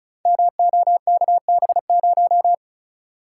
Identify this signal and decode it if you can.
Morse code, heard as a keyed tone, about 34 wpm, 700 Hz. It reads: MOKB0
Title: Code Groups - 5 Letter:Numbers - 35wpm